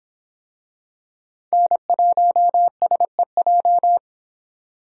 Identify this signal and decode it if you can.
Morse code, heard as a keyed tone, about 26 wpm, 700 Hz. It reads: N1SEJ